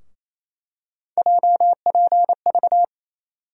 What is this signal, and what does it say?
Morse code, heard as a keyed tone, about 28 wpm, 700 Hz. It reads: JPV
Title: USA Call Sign Postfixes - 3 Letters 28wpm